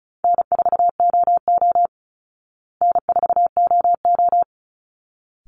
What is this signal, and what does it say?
Morse code, heard as a keyed tone, about 34 wpm, 700 Hz. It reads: N4OO N4OO
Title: Call Signs Repeated 3x - 35wpm